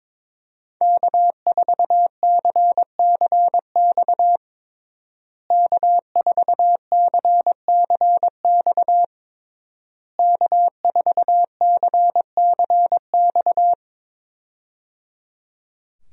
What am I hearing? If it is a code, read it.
Morse code, heard as a keyed tone, about 22 wpm, 690 Hz. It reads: K4CCX K4CCX K4CCX
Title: USA Call Signs - 5 Characters - Repeated 3x - 22wpm